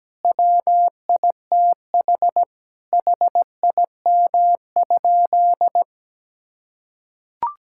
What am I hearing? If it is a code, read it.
Morse code, heard as a keyed tone, about 17 wpm, 700 Hz. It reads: WITH HIM?
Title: Sentences from Top 1000 Words - Review Part 6 - 17wpm